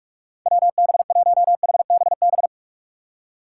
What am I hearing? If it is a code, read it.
Morse code, heard as a keyed tone, about 45 wpm, 700 Hz. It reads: WB1HBB